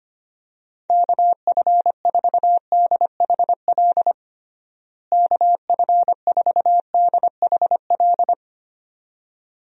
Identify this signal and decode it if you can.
Morse code, heard as a keyed tone, about 25 wpm, 700 Hz. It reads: KF4DHL KF4DHL
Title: USA Call Signs - 6 Characters - Repeated 2x - 25wpm